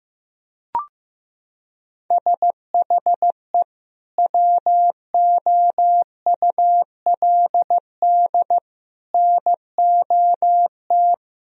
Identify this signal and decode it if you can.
Morse code, heard as a keyed tone, about 15 wpm, 700 Hz. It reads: SHE WOULD NOT
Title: Sentences from Top 1000 Words - Review Part 3 - 15wpm